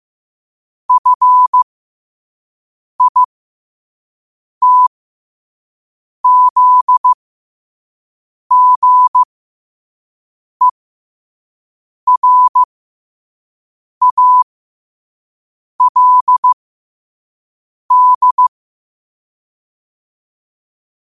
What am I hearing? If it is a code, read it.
Morse code, heard as a keyed tone, about 15 wpm, spaced as for 5 wpm, 1000 Hz. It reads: FITZGERALD
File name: qotdfiles / 5wpm